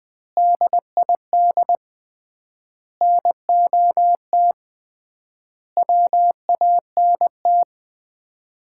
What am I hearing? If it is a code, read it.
Morse code, heard as a keyed tone, about 21 wpm, 700 Hz. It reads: DID NOT WANT
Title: 3x Word-Spacing - Sets of 5 Words - 20wpm